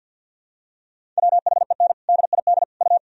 Morse code, heard as a keyed tone, about 53 wpm, 700 Hz. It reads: WHEN DID U
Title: Sentences from Top 300 Words and Abbreviations 50wpm